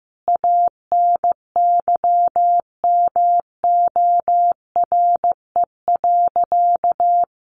Morse code, heard as a keyed tone, about 15 wpm, 700 Hz. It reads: ANYMORE.
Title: Easy Phrases 15wpm